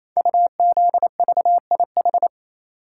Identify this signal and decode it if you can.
Morse code, heard as a keyed tone, about 27 wpm, 700 Hz. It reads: UZVIH
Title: Code Groups - 5 Letters - 28wpm